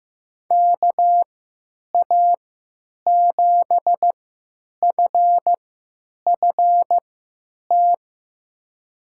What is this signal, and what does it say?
Morse code, heard as a keyed tone, about 15 wpm, spaced as for 9 wpm, 700 Hz. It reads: KA7FFT
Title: USA Call Signs - 3x Character Spacing - 15wpm